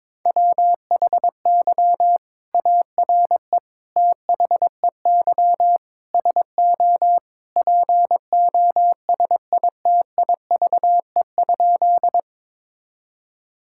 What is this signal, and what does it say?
Morse code, heard as a keyed tone, about 22 wpm, 700 Hz. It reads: WHY ARE THEY SO POSITIVE?
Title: Sentences from Top 1000 Words - Review Part 5 - 22wpm